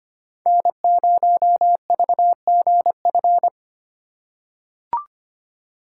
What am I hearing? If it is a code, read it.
Morse code, heard as a keyed tone, about 25 wpm, 700 Hz. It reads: N0VGF
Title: USA Call Signs - 4x Character Spacing - 25wpm